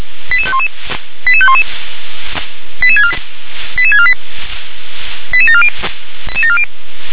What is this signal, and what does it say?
Five-tone selective call (ZVEI), heard as quick runs of five tones, tones of about 70 ms. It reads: 88511 88511 88648 88740
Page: ZVEI